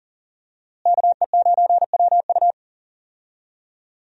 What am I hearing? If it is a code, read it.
Morse code, heard as a keyed tone, about 40 wpm, 700 Hz. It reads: KE9WU
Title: USA Call Signs - 5 Characters - Repeated 2x - 40wpm